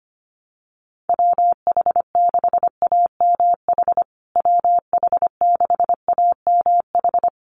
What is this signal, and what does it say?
Morse code, heard as a keyed tone, about 25 wpm, 700 Hz. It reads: WH6AMH WH6AMH
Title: USA Call Signs Repeated 3x 25wpm